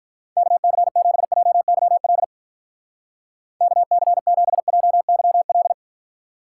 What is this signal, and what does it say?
Morse code, heard as a keyed tone, about 44 wpm, 700 Hz. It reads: KC7JYL KC7JYL